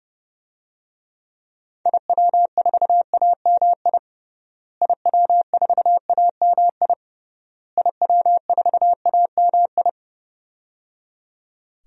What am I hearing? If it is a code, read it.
Morse code, heard as a keyed tone, about 31 wpm, 700 Hz. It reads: IW4AMI IW4AMI IW4AMI
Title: DXCC-100 Call Signs - 6 Characters - Repeated 3x - 30wpm